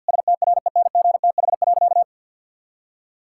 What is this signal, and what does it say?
Morse code, heard as a keyed tone, about 50 wpm, 700 Hz. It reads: STRENGTH.
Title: Sentences from Top 1600 Words 50wpm